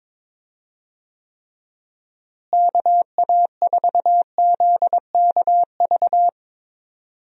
Morse code, heard as a keyed tone, about 21 wpm, 700 Hz. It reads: KA4ZKV
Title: USA Call Signs - 6 Characters - Repeated 3x - 22wpm